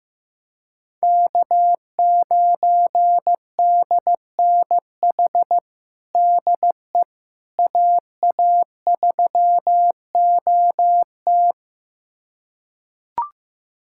Morse code, heard as a keyed tone, about 16 wpm, 700 Hz. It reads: K9DNH DE AA3OT
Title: USA Callsign DE Callsign - 5 Characters - 15wpm